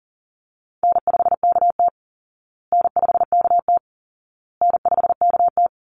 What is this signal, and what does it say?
Morse code, heard as a keyed tone, about 40 wpm, 700 Hz. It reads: N5KT N5KT N5KT
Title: Easy Call Signs Repeated 3x - 40wpm